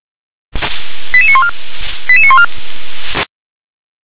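Five-tone selective call (ZVEI), heard as quick runs of five tones, tones of about 70 ms. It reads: 88914 88914
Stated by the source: ZVEI five-tone call received by SDR.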